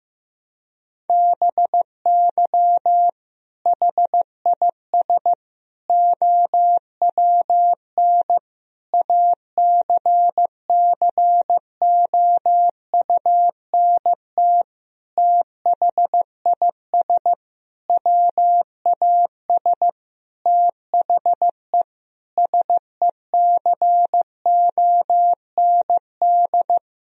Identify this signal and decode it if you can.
Morse code, heard as a keyed tone, about 15 wpm, 700 Hz. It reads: BY HIS OWN ACCOUNT THIS WAS THE SECOND
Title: Sentences from Top 1000 Words - Review Part 1 - 15wpm